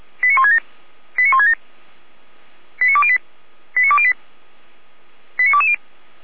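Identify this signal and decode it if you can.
Five-tone selective call (ZVEI), heard as quick runs of five tones, tones of about 70 ms. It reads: 89167 89167 89208 89208 89220
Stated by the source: ZVEI five-tone selective call 89160 received via SDR, from Feuerwehr Orlamünde audio archive.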